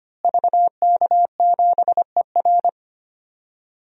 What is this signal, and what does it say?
Morse code, heard as a keyed tone, about 25 wpm, 700 Hz. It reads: VK7ER
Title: DXCC-100 Call Signs - 5 Characters - Repeated 2x - 25wpm